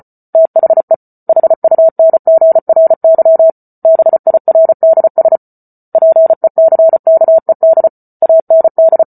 Morse code, heard as a keyed tone, about 34 wpm, 650 Hz. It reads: THE HUNGRY BIRDS PECKED AND
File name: Book Two / 08-Chapter 8 - 34 WPM